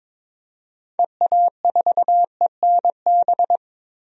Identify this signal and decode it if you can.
Morse code, heard as a keyed tone, about 22 wpm, 700 Hz. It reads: EA4ENB
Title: DXCC-100 Call Signs - 6 Characters - Repeated 3x - 22wpm